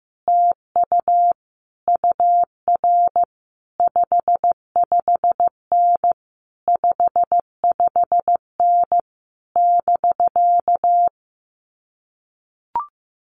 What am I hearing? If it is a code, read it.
Morse code, heard as a keyed tone, about 16 wpm, 700 Hz. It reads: TU UR 55N 55N <BK>
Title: Common QSO Phrases Encore 15wpm